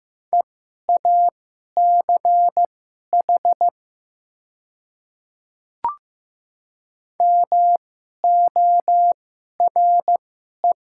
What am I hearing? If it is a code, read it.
Morse code, heard as a keyed tone, about 15 wpm, spaced as for 11 wpm, 700 Hz. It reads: EACH MORE
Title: 2x Character Spacing - Top 100 Words - 15wpm